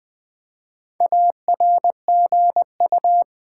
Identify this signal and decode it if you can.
Morse code, heard as a keyed tone, about 20 wpm, 700 Hz. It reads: ARGU
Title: Build A Word - 4 Letters At A Time - 8-Letter Words - 20wpm